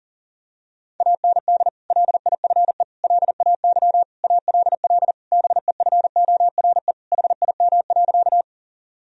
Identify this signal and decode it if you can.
Morse code, heard as a keyed tone, about 40 wpm, 700 Hz. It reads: AND LIFE LAY ALL BEFORE HIM.
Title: Sentences from Top 1000 Words - Review Part 2 - 40wpm